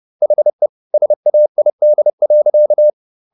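Morse code, heard as a keyed tone, about 30 wpm, 600 Hz. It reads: HE SAID.
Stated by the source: Morse code at 30 wpm, the novel Never by Ken Follett in English, from CW med Gnister.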